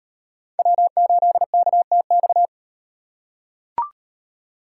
Morse code, heard as a keyed tone, about 38 wpm, 700 Hz. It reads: W8KTX